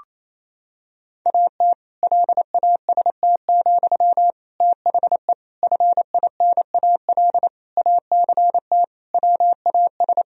Morse code, heard as a keyed tone, about 28 wpm, 700 Hz. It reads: AT LAST, THE FINAL ACT WAS
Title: Sentences from Top 1000 Words - Review Part 2 - 28wpm